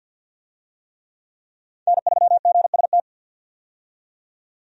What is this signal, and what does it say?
Morse code, heard as a keyed tone, about 52 wpm, 700 Hz. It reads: N3GST
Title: USA Call Signs - 5 Characters - 50wpm